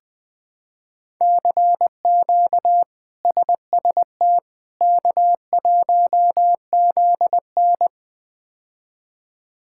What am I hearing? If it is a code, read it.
Morse code, heard as a keyed tone, about 20 wpm, 700 Hz. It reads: CQ SST K1ZN
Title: K1USN SST QSO Practice - 20wpm